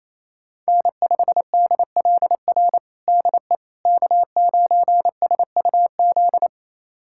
Morse code, heard as a keyed tone, about 28 wpm, 700 Hz. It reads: N5DLR DE K9SUZ
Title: USA Callsign DE Callsign - 5 Characters - 28wpm